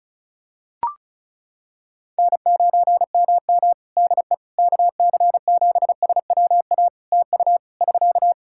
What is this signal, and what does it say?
Morse code, heard as a keyed tone, about 34 wpm, 700 Hz. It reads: N9MM DE KC7SWA TU <SK>
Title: Practice QSO - Exchange 4 - Ending - 35wpm